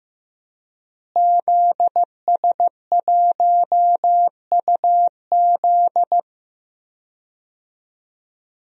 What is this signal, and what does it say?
Morse code, heard as a keyed tone, about 15 wpm, 700 Hz. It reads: ZS1UZ
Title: DXCC-100 Call Signs - 5 Characters - Repeated 2x - 15wpm